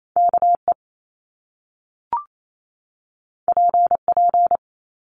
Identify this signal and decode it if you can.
Morse code, heard as a keyed tone, about 28 wpm, 700 Hz. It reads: KE PP
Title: Top Two Letter Combinations in Words 28wpm